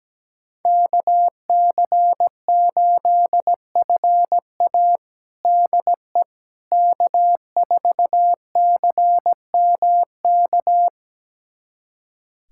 Morse code, heard as a keyed tone, about 17 wpm, 700 Hz. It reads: KC8FA DE K4CMK
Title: USA Callsign DE Callsign - 5 Characters - 17wpm